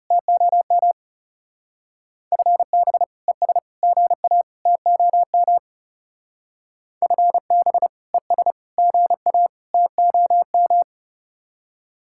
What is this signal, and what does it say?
Morse code, heard as a keyed tone, about 37 wpm, 700 Hz. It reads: TOM FB ES GA TOM FB ES GA TOM
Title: Speed-Racing Common QSO Phrases Encore - 30wpm